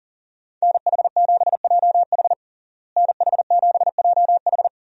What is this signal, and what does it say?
Morse code, heard as a keyed tone, about 40 wpm, 700 Hz. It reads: NH7JH NH7JH
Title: USA Call Signs - 5 Characters - Repeated 2x - 40wpm